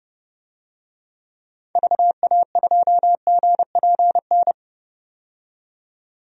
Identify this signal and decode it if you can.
Morse code, heard as a keyed tone, about 32 wpm, 700 Hz. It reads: VA2GPN
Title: DXCC-100 Call Signs - 6 Characters - 30wpm